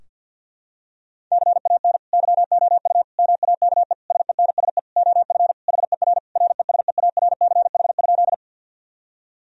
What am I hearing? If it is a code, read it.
Morse code, heard as a keyed tone, about 50 wpm, 700 Hz. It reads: CAN YOU MAKE SENSE OF HER RESEARCH?
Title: Sentences from Top 1000 Words - Review Part 4 - 50wpm